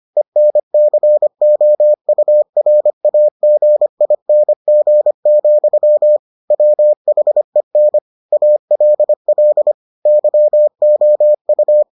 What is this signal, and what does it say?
Morse code, heard as a keyed tone, about 25 wpm, 600 Hz. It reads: ENCOURAGING, WHEN ALL YOU